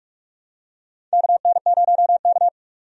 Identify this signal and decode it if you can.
Morse code, heard as a keyed tone, about 44 wpm, 700 Hz. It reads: KN0K
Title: USA Call Signs - 4 Characters - 45wpm